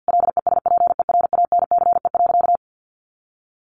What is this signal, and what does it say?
Morse code, heard as a keyed tone, about 50 wpm, 700 Hz. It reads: LESPERANCE.
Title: A Sound Of Thunder 50wpm